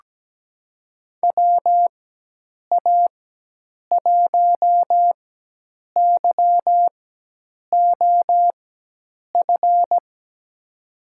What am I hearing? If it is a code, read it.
Morse code, heard as a keyed tone, about 17 wpm, spaced as for 8 wpm, 700 Hz. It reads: WA1YOF
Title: USA Call Signs - 6 Characters - 4x Character Spacing - 17wpm